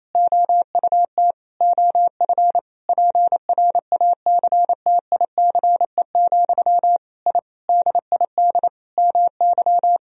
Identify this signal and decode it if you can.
Morse code, heard as a keyed tone, about 28 wpm, 700 Hz. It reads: OUT OF PRACTICE, I DID MY
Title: Sentences from Top 1000 Words - Review Part 4 - 28wpm